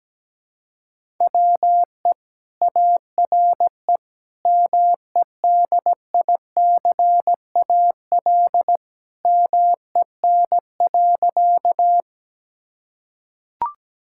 Morse code, heard as a keyed tone, about 17 wpm, 700 Hz. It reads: WE ARE MEDICAL MEN.
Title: Sentences from Top 1000 Words - Review Part 5 - 17wpm